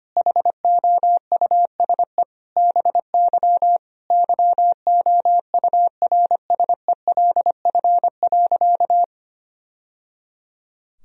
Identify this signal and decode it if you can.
Morse code, heard as a keyed tone, about 25 wpm, 700 Hz. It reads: HOUSE BY YOURSELF.
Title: Sentences from Top 1000 Words - Review Part 4 - 25wpm